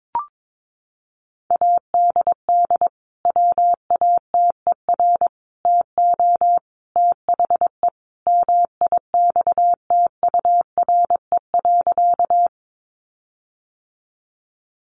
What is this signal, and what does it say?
Morse code, heard as a keyed tone, about 22 wpm, 700 Hz. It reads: ADD WATER TO THE MIXTURE.
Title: Sentences from Top 1500 Words 22wpm